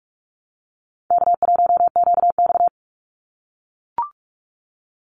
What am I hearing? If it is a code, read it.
Morse code, heard as a keyed tone, about 44 wpm, 700 Hz. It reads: K1QX
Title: Easy Call Signs - 45wpm